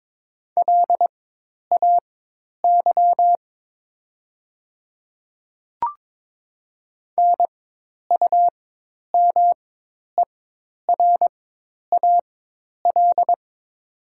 Morse code, heard as a keyed tone, about 22 wpm, spaced as for 10 wpm, 700 Hz. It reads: LAY NUMERAL
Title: 4x Character Spacing - Top 500 Words - 22wpm